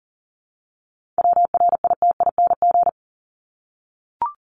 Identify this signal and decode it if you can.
Morse code, heard as a keyed tone, about 40 wpm, 700 Hz. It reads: WRITING
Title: Top 751-1000 Words 40wpm